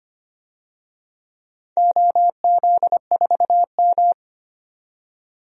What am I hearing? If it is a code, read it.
Morse code, heard as a keyed tone, about 25 wpm, 700 Hz. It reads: OZ4M
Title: DXCC-100 Call Signs - 4 Characters - 25wpm